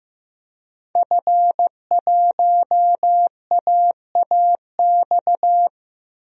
Morse code, heard as a keyed tone, about 16 wpm, 700 Hz. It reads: F1AAX
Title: DXCC-100 Call Signs - 5 Characters - 15wpm